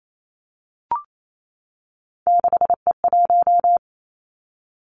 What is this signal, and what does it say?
Morse code, heard as a keyed tone, about 28 wpm, 700 Hz. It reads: 6E1
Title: Three Challenging Letter-Number Combinations 28wpm